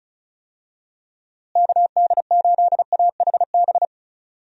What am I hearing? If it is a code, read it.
Morse code, heard as a keyed tone, about 36 wpm, 700 Hz. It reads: KD8AHB
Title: USA Call Signs - 6 Characters - Repeated 2x - 35wpm